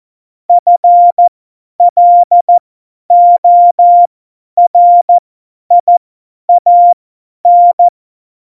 Morse code, heard as a keyed tone, about 14 wpm, spaced as for 10 wpm, 700 Hz. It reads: FLORIAN
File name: Morsekurs / Witze / Witz 11. Juni 2019, 50 F70